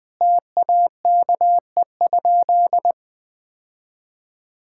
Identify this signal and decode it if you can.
Morse code, heard as a keyed tone, about 21 wpm, 700 Hz. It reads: TAKE?
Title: Sentences from Top 1000 Words - Review Part 6 - 20wpm